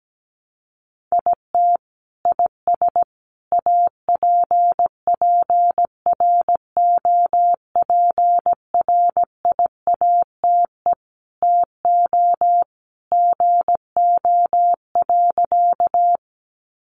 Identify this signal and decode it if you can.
Morse code, heard as a keyed tone, about 17 wpm, 700 Hz. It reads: IT IS APPROPRIATE TO GO.
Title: Sentences from Top 900 Words 17wpm